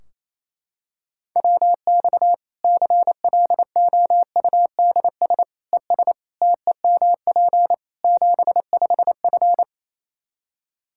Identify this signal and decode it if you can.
Morse code, heard as a keyed tone, about 28 wpm, 700 Hz. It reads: WX CLOUDS ES TEMP 75F
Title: K1USN SST QSO Practice 28wpm